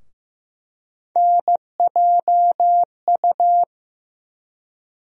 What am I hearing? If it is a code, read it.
Morse code, heard as a keyed tone, about 15 wpm, 700 Hz. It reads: NJU
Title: USA Call Sign Postfixes - 3 Letters 15wpm